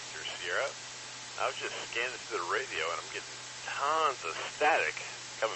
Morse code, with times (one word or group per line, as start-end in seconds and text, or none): none